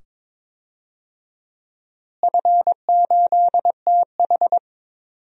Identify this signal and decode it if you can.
Morse code, heard as a keyed tone, about 22 wpm, 700 Hz. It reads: F8TH
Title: DXCC-100 Call Signs - 4 Characters - Repeated 2x - 22wpm